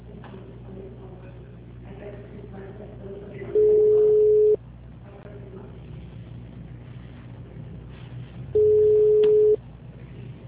Ringback tone (Europe, 425 Hz).